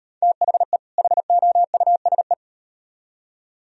Morse code, heard as a keyed tone, about 38 wpm, 700 Hz. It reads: THE HOUSE